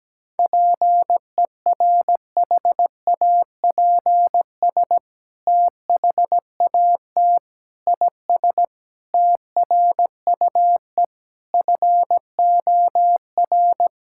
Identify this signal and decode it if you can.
Morse code, heard as a keyed tone, about 17 wpm, 700 Hz. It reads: PERHAPS THAT IS TRUE FOR
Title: Sentences from Top 1000 Words - Review Part 5 - 17wpm